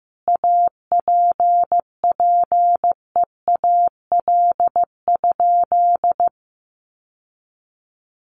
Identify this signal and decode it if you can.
Morse code, heard as a keyed tone, about 15 wpm, 700 Hz. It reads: APPEAL?
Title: Sentences from Top 1300 Words 15wpm